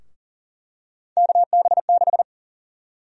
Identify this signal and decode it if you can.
Morse code, heard as a keyed tone, about 37 wpm, 700 Hz. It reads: KB6
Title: USA Call Sign Prefixes - 40wpm